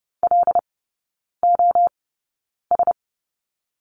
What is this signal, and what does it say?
Morse code, heard as a keyed tone, about 30 wpm, spaced as for 9 wpm, 690 Hz. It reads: LOS
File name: Lesson 17e - Letter u - Three Characters 30wpm